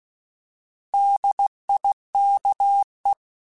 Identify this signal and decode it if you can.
Morse code, heard as a keyed tone, about 16 wpm, 800 Hz. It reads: DIKE
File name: Alphabetical / Alphabetical mp3s / Ds @15